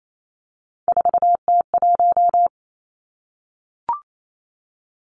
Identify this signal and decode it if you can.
Morse code, heard as a keyed tone, about 28 wpm, 700 Hz. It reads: 4T1
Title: Three Challenging Letter-Number Combinations 28wpm